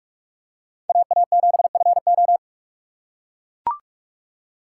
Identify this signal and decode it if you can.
Morse code, heard as a keyed tone, about 44 wpm, 700 Hz. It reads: AA7FO